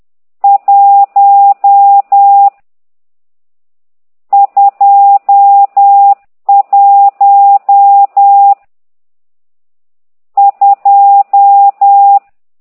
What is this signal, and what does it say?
Morse code, heard as a keyed tone, about 10 wpm, 800 Hz. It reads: 1 2 1 2
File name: ct2idw / exerc12